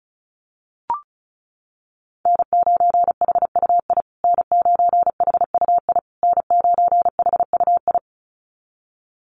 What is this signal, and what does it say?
Morse code, heard as a keyed tone, about 34 wpm, 700 Hz. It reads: N9HUI N9HUI N9HUI
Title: USA Call Signs Repeated 3x 35wpm